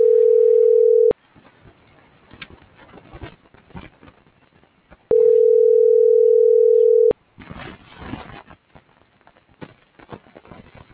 Ringback tone (North America), 440 + 480 Hz.